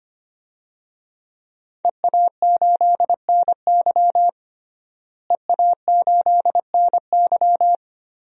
Morse code, heard as a keyed tone, about 25 wpm, 700 Hz. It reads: EA8NY EA8NY
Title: DXCC-100 Call Signs - 5 Characters - Repeated 2x - 25wpm